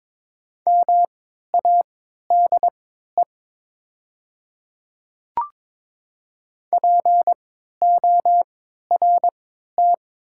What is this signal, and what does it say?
Morse code, heard as a keyed tone, about 22 wpm, spaced as for 12 wpm, 700 Hz. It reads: MADE PORT
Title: 3x Character Spacing - Top 500 Words - 22wpm